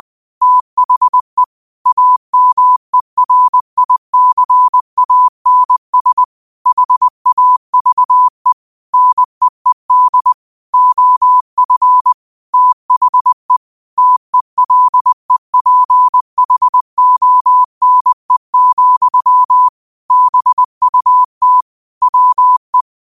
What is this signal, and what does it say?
Morse code, heard as a keyed tone, about 20 wpm, 1000 Hz. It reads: THE AMERICANS HAVE NEED OF THE TELEPHONE, BUT WE
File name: qotdfiles / 20wpm